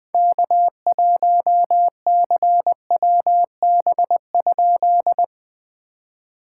Morse code, heard as a keyed tone, about 20 wpm, 700 Hz. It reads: K1CWB?
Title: POTA QSO Practice - 20wpm